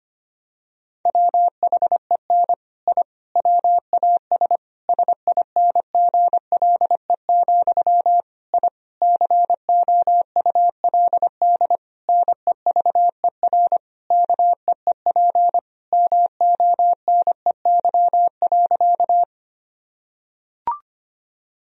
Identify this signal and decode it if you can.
Morse code, heard as a keyed tone, about 25 wpm, 700 Hz. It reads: WHEN I WAS SINGLE, I COULD NEVER KEEP MONEY.
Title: Sentences from Top 1000 Words - Review Part 4 - 25wpm